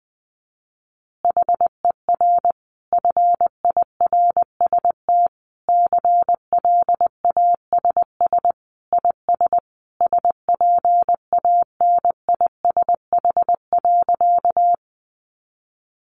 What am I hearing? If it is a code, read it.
Morse code, heard as a keyed tone, about 20 wpm, 700 Hz. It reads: HER FIRST CLASS IS SPANISH.
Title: Additional Easy Phrases 20wpm